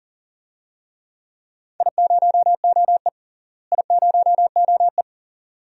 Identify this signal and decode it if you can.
Morse code, heard as a keyed tone, about 40 wpm, 700 Hz. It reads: I0OE I0OE
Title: DXCC-100 Call Signs - 4 Characters - Repeated 2x - 40wpm